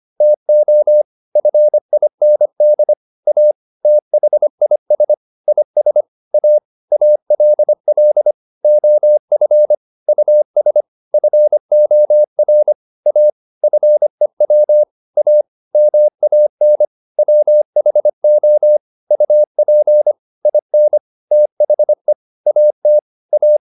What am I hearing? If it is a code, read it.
Morse code, heard as a keyed tone, about 25 wpm, 600 Hz. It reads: TO FIND A THIS IS A ALL OF US FOR A FEW A MAN WHO UP IN THE AT A
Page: advanced-practice-files